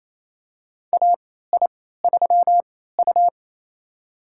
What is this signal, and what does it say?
Morse code, heard as a keyed tone, about 28 wpm, spaced as for 16 wpm, 700 Hz. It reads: AI3U
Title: USA Call Signs - 4 Characters - 3x Character Spacing - 28wpm